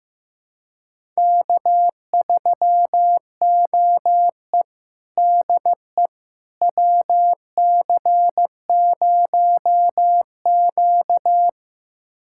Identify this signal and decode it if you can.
Morse code, heard as a keyed tone, about 15 wpm, 700 Hz. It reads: K3OE DE WC0Q
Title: USA Callsign DE Callsign - 4 Characters - 15wpm